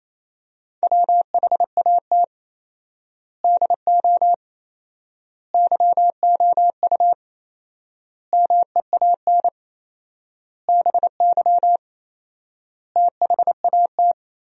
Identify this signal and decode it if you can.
Morse code, heard as a keyed tone, about 28 wpm, 700 Hz. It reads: WHAT DO YOU MEAN BY THAT
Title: 4x Word-Spacing - Sentences from Top 400 Words - 28wpm